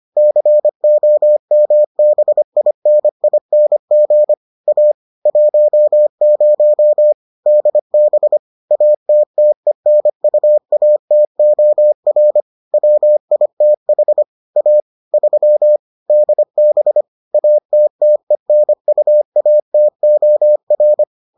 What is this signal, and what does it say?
Morse code, heard as a keyed tone, about 25 wpm, 600 Hz. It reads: COMBINING A 10 DB ATTENUATOR WITH A 3 DB ATTENUATOR